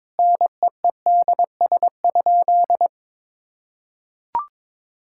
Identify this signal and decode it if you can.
Morse code, heard as a keyed tone, about 22 wpm, 700 Hz. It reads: NEEDS?
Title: Sentences from Top 1000 Words - Review Part 5 - 22wpm